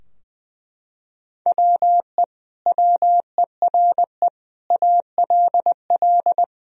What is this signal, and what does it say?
Morse code, heard as a keyed tone, about 20 wpm, 700 Hz. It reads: WE WERE ALL
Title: Sentences from Top 1000 Words - Review Part 4 - 20wpm